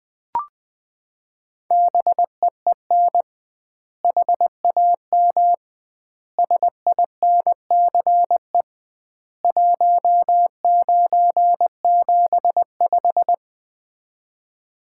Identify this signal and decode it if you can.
Morse code, heard as a keyed tone, about 21 wpm, 700 Hz. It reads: BEEN HAM SINCE 1975
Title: Common QSO Phrases - 2x Word-Spacing - 20wpm